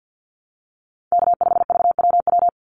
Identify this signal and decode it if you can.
Morse code, heard as a keyed tone, about 48 wpm, 700 Hz. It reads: K5VWW
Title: Call Signs - 50wpm